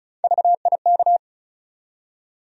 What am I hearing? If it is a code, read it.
Morse code, heard as a keyed tone, about 34 wpm, 700 Hz. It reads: VIK
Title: USA Call Sign Postfixes - 3 Letters 35wpm